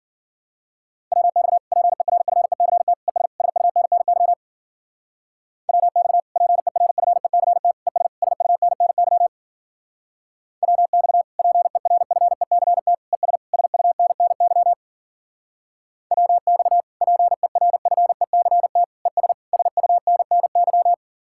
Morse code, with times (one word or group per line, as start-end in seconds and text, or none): 1.1-1.6 WX
1.7-2.9 PERFECT
3.1-3.3 ES
3.4-4.3 SUNNY
5.7-6.2 WX
6.4-7.7 PERFECT
7.9-8.1 ES
8.2-9.3 SUNNY
10.6-11.2 WX
11.4-12.9 PERFECT
13.1-13.4 ES
13.5-14.7 SUNNY
16.1-16.8 WX
17.0-18.8 PERFECT
19.1-19.3 ES
19.5-20.9 SUNNY